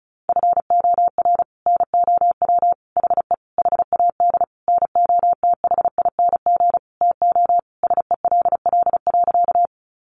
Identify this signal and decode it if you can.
Morse code, heard as a keyed tone, about 34 wpm, 700 Hz. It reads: FOR NOW HE HAD NOTHING TO SELL.
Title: Sentences from Top 1100 Words 35wpm